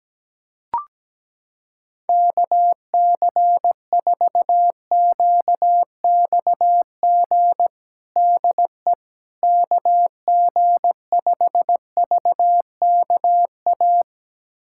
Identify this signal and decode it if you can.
Morse code, heard as a keyed tone, about 17 wpm, 700 Hz. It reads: KC4QXG DE KG5VKA
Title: USA Callsign DE Callsign - 6 Characters - 17wpm